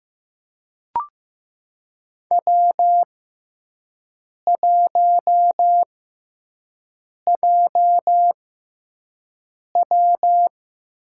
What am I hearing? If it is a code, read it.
Morse code, heard as a keyed tone, about 15 wpm, spaced as for 5 wpm, 700 Hz. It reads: W1JW
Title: USA Call Signs - 4 Characters - 6x Character Spacing - 15wpm